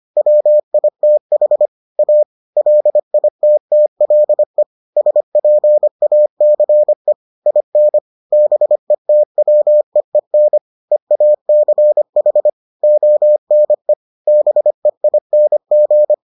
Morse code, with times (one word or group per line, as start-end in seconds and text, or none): 0.2-1.7 WITH
2.0-2.2 A
2.6-4.6 LITTLE
5.0-7.1 SPACE
7.5-8.0 IN
8.3-10.6 BETWEEN
10.9-12.5 EACH
12.8-13.9 ONE
14.3-16.1 BEING